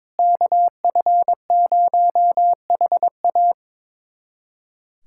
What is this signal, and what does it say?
Morse code, heard as a keyed tone, about 22 wpm, 700 Hz. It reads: KF0HA
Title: USA Callsign DE Callsign 22wpm